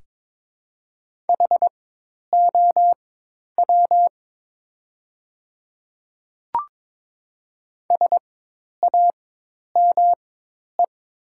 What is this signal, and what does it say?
Morse code, heard as a keyed tone, about 22 wpm, spaced as for 10 wpm, 700 Hz. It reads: HOW SAME
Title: 4x Character Spacing - Top 200 Words - 22wpm